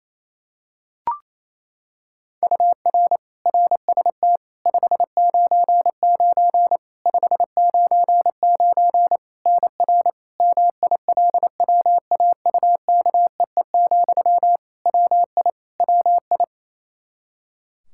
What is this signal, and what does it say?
Morse code, heard as a keyed tone, about 28 wpm, 700 Hz. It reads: UR RST 599 599 NR MILWAUKEE, WI WI
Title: Practice QSO - Exchange 1 - Beginning - 28wpm